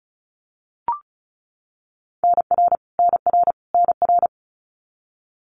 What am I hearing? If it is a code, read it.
Morse code, heard as a keyed tone, about 34 wpm, 700 Hz. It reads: NR NR NR
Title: Abbreviations Repeated 3x - 35wpm